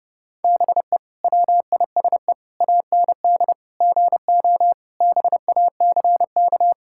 Morse code, heard as a keyed tone, about 32 wpm, 700 Hz. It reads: BE WISE AND GO BACK
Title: Sentences from Top 1600 Words 30wpm